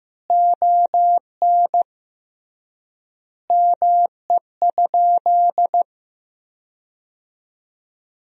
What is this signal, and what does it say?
Morse code, heard as a keyed tone, about 15 wpm, 700 Hz. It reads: ON ME?
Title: 3x Word-Spacing - Sentences from Top 500 Words - 15wpm